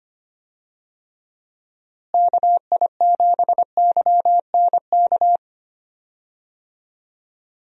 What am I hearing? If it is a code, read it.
Morse code, heard as a keyed tone, about 25 wpm, 700 Hz. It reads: KI7YNK
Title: USA Call Signs - 6 Characters - Repeated 3x - 25wpm